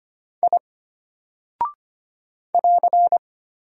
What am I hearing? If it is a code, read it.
Morse code, heard as a keyed tone, about 25 wpm, 700 Hz. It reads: I <AR>
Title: Single Character (Letter:Number:Punctuation:Prosign) Alt 25wpm